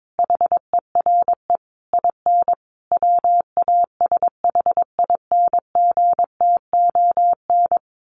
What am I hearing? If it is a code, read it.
Morse code, heard as a keyed tone, about 22 wpm, 700 Hz. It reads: HERE IN WASHINGTON
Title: Sets of 5 Words Encore2 22wpm